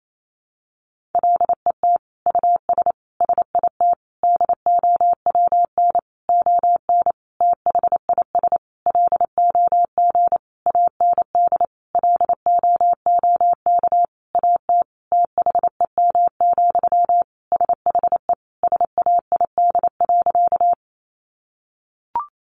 Morse code, heard as a keyed tone, about 28 wpm, 700 Hz. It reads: LET US SIT DOWN ON THIS LOG AND LOOK AT THEM, SHE SAID.
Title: Sentences from Top 1400 Words 28wpm